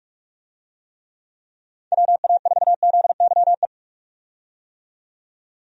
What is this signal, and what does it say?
Morse code, heard as a keyed tone, about 46 wpm, 700 Hz. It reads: WA4ZYE
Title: USA Call Signs - 6 Characters - 45wpm